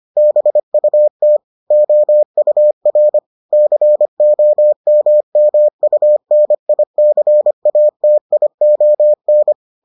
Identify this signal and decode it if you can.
Morse code, heard as a keyed tone, about 25 wpm, 600 Hz. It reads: BUT OUR COMMUNICATION